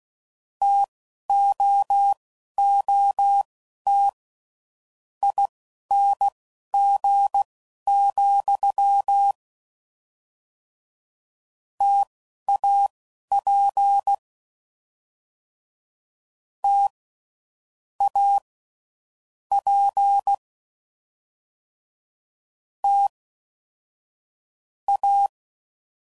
Morse code, heard as a keyed tone, about 16 wpm, spaced as for 12 wpm, 800 Hz. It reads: TOOT ING, TAP T A P T A
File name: Mixed / Variable Sp / T AOEIs